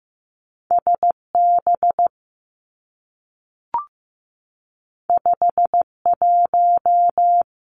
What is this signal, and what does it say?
Morse code, heard as a keyed tone, about 15 wpm, 700 Hz. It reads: S B 5 1
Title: Two Challenging Letter-Number Combinations 15wpm